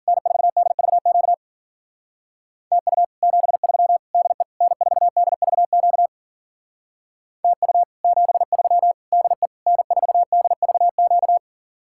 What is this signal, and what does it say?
Morse code, heard as a keyed tone, about 52 wpm, 700 Hz. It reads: N4DVQ TU 73 DE N4DVQ TU 73 DE N4DVQ